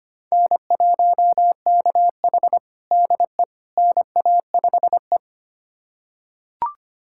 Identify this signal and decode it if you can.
Morse code, heard as a keyed tone, about 25 wpm, 700 Hz. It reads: N1KH DE NA5E
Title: USA Callsign DE Callsign - 4 Characters - 25wpm